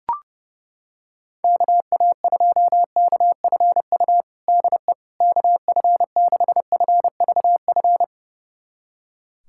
Morse code, heard as a keyed tone, about 32 wpm, 700 Hz. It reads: KA2KFU DE KF6FVF
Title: USA Callsign DE Callsign - 6 Characters - 30wpm